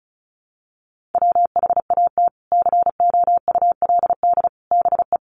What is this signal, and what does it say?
Morse code, heard as a keyed tone, about 34 wpm, 700 Hz. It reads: WHAT COULD BE
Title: Sentences from Top 1000 Words 35wpm